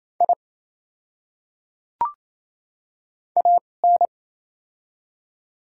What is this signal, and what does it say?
Morse code, heard as a keyed tone, about 28 wpm, spaced as for 20 wpm, 700 Hz. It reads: I AN
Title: 2x Character Spacing - Top 100 Words - 28wpm